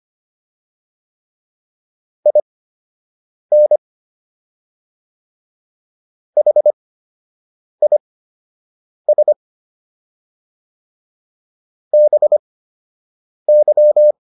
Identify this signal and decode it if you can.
Morse code, heard as a keyed tone, about 25 wpm, spaced as for 7 wpm, 600 Hz. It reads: IN HIS BY